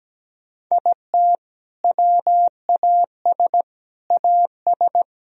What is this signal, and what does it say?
Morse code, heard as a keyed tone, about 17 wpm, 700 Hz. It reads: IT WAS AS
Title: Sentences from Top 1000 Words - Review Part 3 - 17wpm